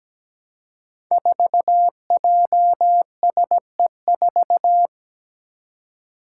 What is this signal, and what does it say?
Morse code, heard as a keyed tone, about 17 wpm, 700 Hz. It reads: 4JSE4
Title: Five Challenging Letter-Number Combinations 17wpm